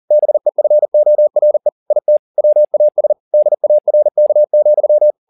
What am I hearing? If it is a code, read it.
Morse code, heard as a keyed tone, about 40 wpm, 600 Hz. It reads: BEFORE IT WAS DARK,
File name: the old man and the sea 2 40wpm